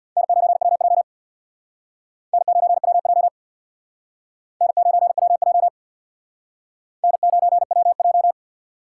Morse code, heard as a keyed tone, about 60 wpm, 700 Hz. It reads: N9WJ N9WJ N9WJ N9WJ
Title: Speed Racing - USA 1x2 Call Signs 50wpm